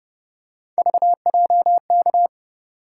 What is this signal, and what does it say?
Morse code, heard as a keyed tone, about 30 wpm, 700 Hz. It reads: VJK
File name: Speed-Racing - Three Letter-Number Combinations - 25wpm